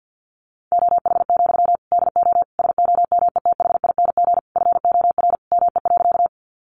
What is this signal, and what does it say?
Morse code, heard as a keyed tone, about 50 wpm, 700 Hz. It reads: OH, DO SOMETHING FOR ME.
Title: Sentences from Top 600 Words 50wpm